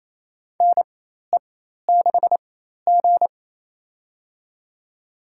Morse code, heard as a keyed tone, about 28 wpm, spaced as for 13 wpm, 700 Hz. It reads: NE6G
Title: USA Call Signs - 4 Characters - 4x Character Spacing - 28wpm